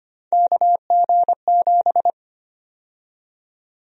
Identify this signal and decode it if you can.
Morse code, heard as a keyed tone, about 25 wpm, 700 Hz. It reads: KG7
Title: USA Call Sign Prefixes - 25wpm